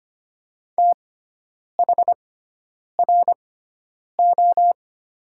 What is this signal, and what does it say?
Morse code, heard as a keyed tone, about 25 wpm, spaced as for 9 wpm, 700 Hz. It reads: THRO
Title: 6x Character Spacing - Top 500 Words - 25wpm